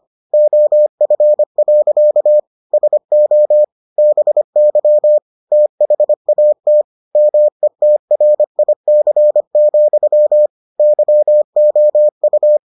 Morse code, heard as a keyed tone, about 25 wpm, 600 Hz. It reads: OF. SO BY THAT METRIC, YOU